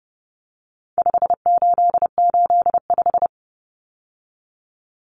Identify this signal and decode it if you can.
Morse code, heard as a keyed tone, about 30 wpm, 700 Hz. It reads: 5885
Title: Numbers - 4-Digit 30wpm